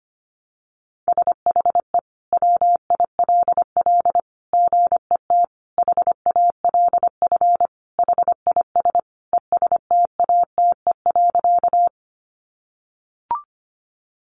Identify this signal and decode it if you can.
Morse code, heard as a keyed tone, about 25 wpm, 700 Hz. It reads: SHE WILL GET HALF HIS ESTATE.
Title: Sentences from Top 1200 Words 25wpm